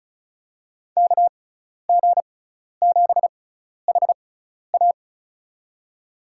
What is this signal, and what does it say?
Morse code, heard as a keyed tone, about 36 wpm, spaced as for 12 wpm, 700 Hz. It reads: KG7HA
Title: USA Call Signs - 5 Characters - 6x Character Spacing - 35wpm